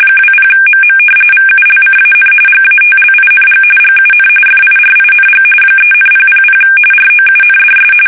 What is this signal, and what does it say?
RTTY, about 45.45 baud. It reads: 20MM VULCAN GATLING GUN FIRING 3,000 ROUNDS
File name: fsk850 / battleshipnj